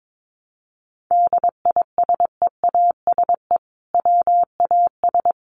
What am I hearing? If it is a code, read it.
Morse code, heard as a keyed tone, about 22 wpm, 700 Hz. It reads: DISEASE WAS
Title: Sentences from Top 1100 Words 22wpm